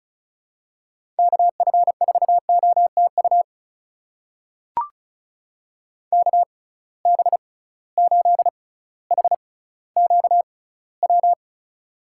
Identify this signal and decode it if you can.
Morse code, heard as a keyed tone, about 36 wpm, spaced as for 18 wpm, 700 Hz. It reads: KF4OTU KB8HQW
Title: USA Call Signs - 6 Characters - 6x Character Spacing - 35wpm